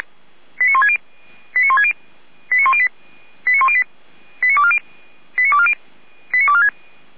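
Five-tone selective call (ZVEI), heard as quick runs of five tones, tones of about 70 ms. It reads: 89160 89160 89108 89108 89240 89240 89246
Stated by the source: Five-tone ZVEI selective call received on an SDR.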